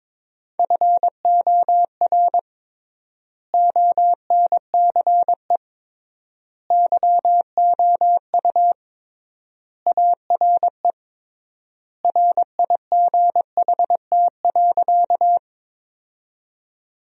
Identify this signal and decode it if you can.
Morse code, heard as a keyed tone, about 22 wpm, 700 Hz. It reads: FOR ONCE YOU ARE RIGHT.
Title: 3x Word-Spacing - Sentences from Top 300 Words - 22wpm